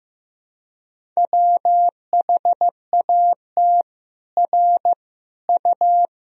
Morse code, heard as a keyed tone, about 15 wpm, 700 Hz. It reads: WHAT R U
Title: Sentences from Top 100 Words and Abbreviations 15wpm